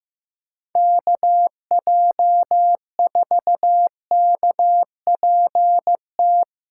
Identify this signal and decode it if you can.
Morse code, heard as a keyed tone, about 15 wpm, 700 Hz. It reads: KJ4KPT
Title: USA Callsign DE Callsign - 6 Characters - 15wpm